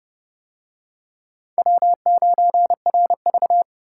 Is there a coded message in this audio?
Yes: Morse code from 1.6 to 3.6 s, heard as a keyed tone, about 29 wpm, 700 Hz. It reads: W9RV